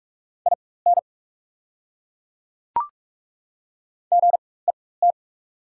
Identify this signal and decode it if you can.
Morse code, heard as a keyed tone, about 48 wpm, spaced as for 21 wpm, 700 Hz. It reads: IN GET
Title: 4x Character Spacing - Top 200 Words - 45wpm